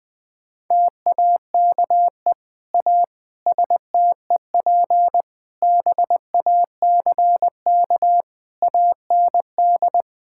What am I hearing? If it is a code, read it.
Morse code, heard as a keyed tone, about 20 wpm, 700 Hz. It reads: TAKE A STEP BACK AND
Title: Sets of 8 Words - 20wpm